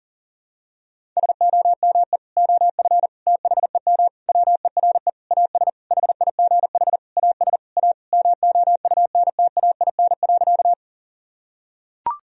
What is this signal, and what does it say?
Morse code, heard as a keyed tone, about 40 wpm, 700 Hz. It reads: SOME OF THEM WERE AS HIGH AS A MOUNTAIN.
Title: Sentences from Top 1000 Words - Review Part 1 - 40wpm